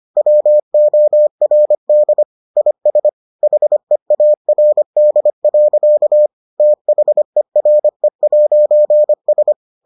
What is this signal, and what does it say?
Morse code, heard as a keyed tone, about 25 wpm, 600 Hz. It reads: WORD IS HEARD. THERE'S